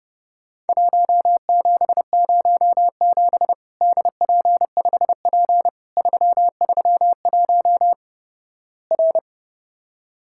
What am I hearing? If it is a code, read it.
Morse code, heard as a keyed tone, about 32 wpm, 700 Hz. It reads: 1707 DP5P 331 R
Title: Worked All Europe DX Contest - QTC Exchange - 30wpm